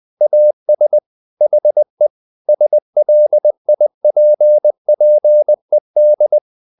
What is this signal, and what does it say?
Morse code, heard as a keyed tone, about 20 wpm, 600 Hz. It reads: AS HE SLIPPED
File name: the old man and the sea part 2 20wpm